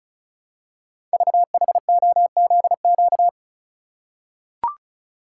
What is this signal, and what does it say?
Morse code, heard as a keyed tone, about 36 wpm, 690 Hz. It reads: VHOZQ
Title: Code Groups - 5 Letters - 35wpm